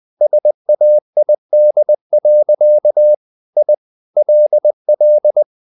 Morse code, heard as a keyed tone, about 20 wpm, 600 Hz. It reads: SAID. I LL
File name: the old man and the sea part 2 20wpm